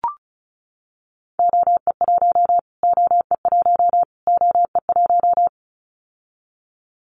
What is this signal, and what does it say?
Morse code, heard as a keyed tone, about 34 wpm, 700 Hz. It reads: OE1 OE1 OE1
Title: Callsign Prefixes for Top 25 DXCC - Repeated 3x - 35wpm